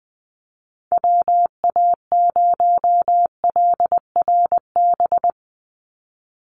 Morse code, heard as a keyed tone, about 20 wpm, 700 Hz. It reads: WA0LRB
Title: USA Call Signs 20wpm